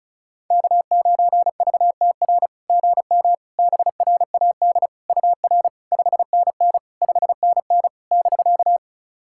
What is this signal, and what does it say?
Morse code, heard as a keyed tone, about 36 wpm, 700 Hz. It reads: K9VTR GM BRAD UR 5NN 5NN <BK>
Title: SOTA QSO Practice 35wpm